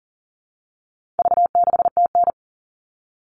Morse code, heard as a keyed tone, about 40 wpm, 700 Hz. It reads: V6TN
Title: All Cut Numbers - 4-Digit 40wpm